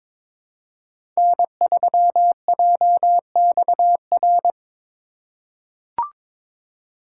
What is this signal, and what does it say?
Morse code, heard as a keyed tone, about 22 wpm, 700 Hz. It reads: N3JXR
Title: USA Call Signs - 2x Character Spacing - 22wpm